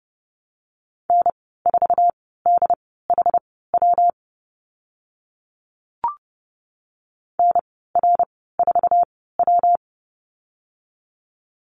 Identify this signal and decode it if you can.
Morse code, heard as a keyed tone, about 30 wpm, spaced as for 17 wpm, 700 Hz. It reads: N4DHW NR4W
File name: Lesson 19g - Letter w - Call Signs 30wpm